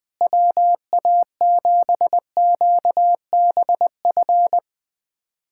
Morse code, heard as a keyed tone, about 20 wpm, 700 Hz. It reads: WA7QBF
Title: USA Callsign DE Callsign 20wpm